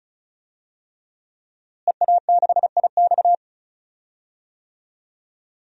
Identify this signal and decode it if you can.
Morse code, heard as a keyed tone, about 36 wpm, 700 Hz. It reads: EA6IX
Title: DXCC-100 Call Signs - 5 Characters - Repeated 3x - 35wpm